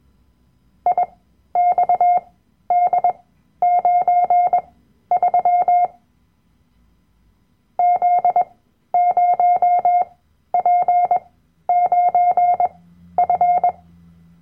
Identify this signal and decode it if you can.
Morse code, heard as a keyed tone, about 21 wpm, spaced as for 12 wpm, 710 Hz. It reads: IXD93 Z0P9F
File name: Workshop Dateien / CW-Online-Kurs / Lektion 010